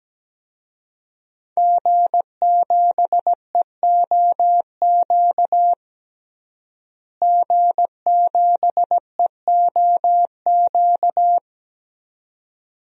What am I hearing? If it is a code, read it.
Morse code, heard as a keyed tone, about 17 wpm, 700 Hz. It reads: G7EOQ G7EOQ
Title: DXCC-100 Call Signs - 5 Characters - Repeated 2x - 17wpm